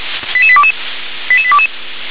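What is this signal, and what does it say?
Five-tone selective call (ZVEI), heard as quick runs of five tones, tones of about 70 ms. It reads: 88822 88822
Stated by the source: ZVEI five-tone call 88740 received by SDR.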